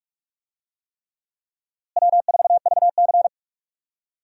Morse code, heard as a keyed tone, about 41 wpm, 700 Hz. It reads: W4VC
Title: USA Call Signs - 4 Characters - Repeated 3x - 45wpm